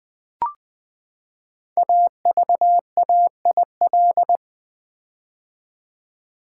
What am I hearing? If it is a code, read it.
Morse code, heard as a keyed tone, about 20 wpm, 700 Hz. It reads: AVAIL
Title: Build A Word - 5 Letters At A Time - 12-Letter Words - 20wpm